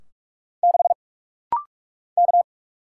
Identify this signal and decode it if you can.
Morse code, heard as a keyed tone, about 44 wpm, 700 Hz. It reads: 6 K
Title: Single Letter-Number Alt - Rapid-Fire 45wpm